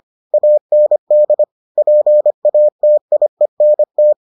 Morse code, heard as a keyed tone, about 25 wpm, 600 Hz. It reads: AND PATIENT